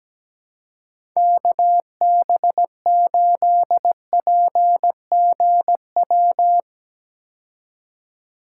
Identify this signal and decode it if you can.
Morse code, heard as a keyed tone, about 17 wpm, 700 Hz. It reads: KB8PGW
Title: K1USN SST QSO Practice - 17wpm